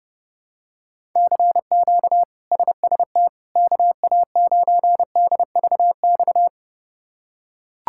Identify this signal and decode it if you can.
Morse code, heard as a keyed tone, about 32 wpm, 700 Hz. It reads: CQ SST KA9DVX
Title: K1USN SST QSO Practice - 30wpm